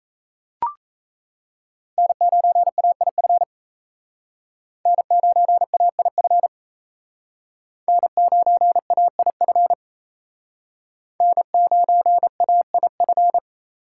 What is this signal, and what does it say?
Morse code, heard as a keyed tone, about 37 wpm, 700 Hz. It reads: N9AIF N9AIF N9AIF N9AIF